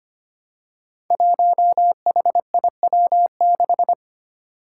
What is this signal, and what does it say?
Morse code, heard as a keyed tone, about 25 wpm, 700 Hz. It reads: 1HIW6
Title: Five Challenging Letter-Number Combinations 25wpm